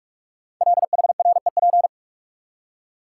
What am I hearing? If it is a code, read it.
Morse code, heard as a keyed tone, about 44 wpm, 700 Hz. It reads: LHREP
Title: Code Groups - 5 Letters - 45wpm